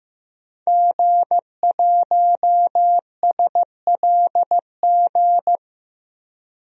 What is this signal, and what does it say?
Morse code, heard as a keyed tone, about 15 wpm, 700 Hz. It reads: G1SLG
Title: DXCC-100 Call Signs - 5 Characters - Repeated 2x - 15wpm